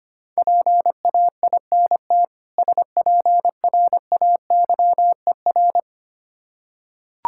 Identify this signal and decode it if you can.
Morse code, heard as a keyed tone, about 25 wpm, 700 Hz. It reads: PAINT SPRAYER
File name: practice / 25wpm / Difficult Words with Hints - 25wpm